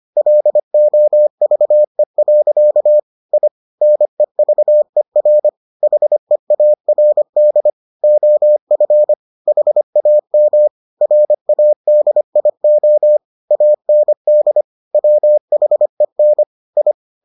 Morse code, heard as a keyed tone, about 25 wpm, 600 Hz. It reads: LOVE. I NEVER HEARD OF HAM RADIO AND WHEN I